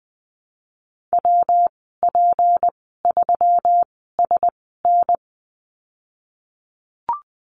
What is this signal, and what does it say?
Morse code, heard as a keyed tone, about 20 wpm, spaced as for 15 wpm, 700 Hz. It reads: WP3SN
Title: 154 Lesson 25g - Letter p - Call Signs 20wpm